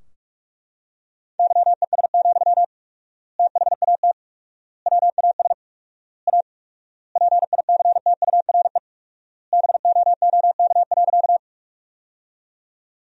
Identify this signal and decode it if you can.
Morse code, heard as a keyed tone, about 42 wpm, 690 Hz. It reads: YES, THAT WAS A PICTURE BOOK.
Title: 4x Word-Spacing - Sentences from Top 500 Words - 45wpm